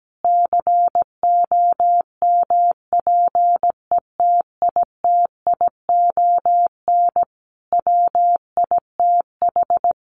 Morse code, heard as a keyed tone, about 17 wpm, 700 Hz. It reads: COMPETITION WITH
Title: Sentences from Top 1000 Words 17wpm